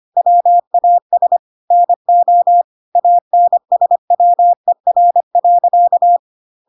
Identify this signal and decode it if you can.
Morse code, heard as a keyed tone, about 25 wpm, 700 Hz. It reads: WAS NO ANSWER.
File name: Pinocchio / Pinocchio ch3 25 25 ik1hge